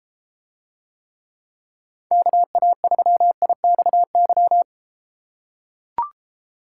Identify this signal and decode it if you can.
Morse code, heard as a keyed tone, about 32 wpm, 700 Hz. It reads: KA3IXY